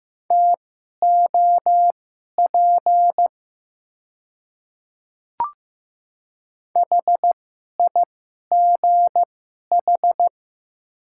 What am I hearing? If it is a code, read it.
Morse code, heard as a keyed tone, about 15 wpm, spaced as for 11 wpm, 700 Hz. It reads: TOP HIGH
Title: 2x Character Spacing - Top 500 Words - 15wpm